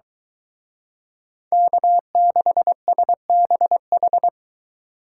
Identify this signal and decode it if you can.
Morse code, heard as a keyed tone, about 23 wpm, 700 Hz. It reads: K6SBH